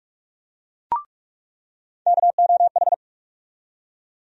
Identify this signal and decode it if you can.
Morse code, heard as a keyed tone, about 44 wpm, 700 Hz. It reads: KOH
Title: USA Call Sign Postfixes - 3 Letters 45wpm